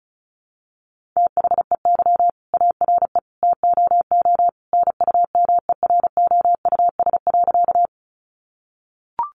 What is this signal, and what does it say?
Morse code, heard as a keyed tone, about 34 wpm, 700 Hz. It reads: THEY ARE TOO NUMEROUS.
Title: Sentences from Top 1400 Words 35wpm